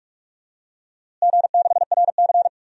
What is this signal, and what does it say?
Morse code, heard as a keyed tone, about 46 wpm, 700 Hz. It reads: G6RC
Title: DXCC-100 Call Signs - 4 Characters - Repeated 2x - 45wpm